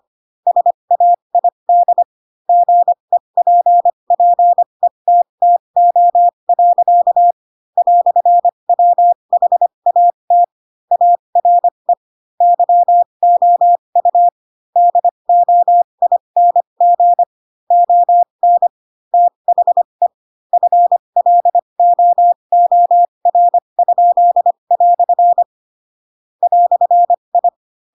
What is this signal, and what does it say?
Morse code, heard as a keyed tone, about 25 wpm, spaced as for 22 wpm, 700 Hz. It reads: SAID GEPPETTO. "WHAT ARE YOU DOING ON THE FLOOR?" "I
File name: Pinocchio / Pinocchio ch2 18 25 ik1hge